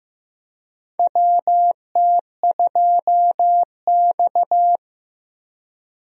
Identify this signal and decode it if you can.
Morse code, heard as a keyed tone, about 15 wpm, 700 Hz. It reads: WT2X
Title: USA Call Signs - 4 Characters - Repeated 2x - 15wpm